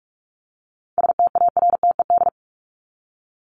Morse code, heard as a keyed tone, about 44 wpm, 700 Hz. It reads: STARTED
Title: ED Suffix - Top 100 Words with Base Words 45wpm